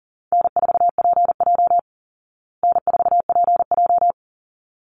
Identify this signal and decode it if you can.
Morse code, heard as a keyed tone, about 40 wpm, 700 Hz. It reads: N4PJ N4PJ
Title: Call Signs Repeated 3x - 40wpm